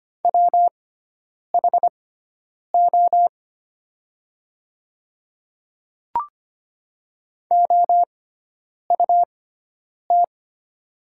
Morse code, heard as a keyed tone, about 25 wpm, spaced as for 9 wpm, 700 Hz. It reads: WHO OUT
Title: 6x Character Spacing - Top 300 Words - 25wpm